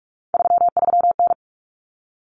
Morse code, heard as a keyed tone, about 44 wpm, 700 Hz. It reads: 33N
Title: Common Cut Numbers - 3-Digit 45wpm